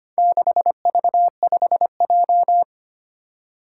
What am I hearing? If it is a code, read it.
Morse code, heard as a keyed tone, about 25 wpm, 700 Hz. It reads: 6V5J
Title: Four Challenging Letter-Number Combinations 25wpm